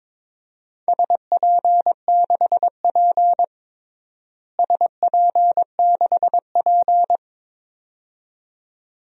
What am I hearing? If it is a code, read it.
Morse code, heard as a keyed tone, about 21 wpm, 700 Hz. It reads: SP6P SP6P
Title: DXCC-100 Call Signs - 4 Characters - Repeated 2x - 22wpm